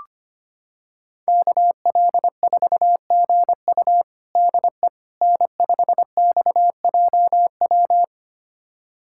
Morse code, heard as a keyed tone, about 25 wpm, 700 Hz. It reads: KL4GU DE N5XJW
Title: USA Callsign DE Callsign - 5 Characters - 25wpm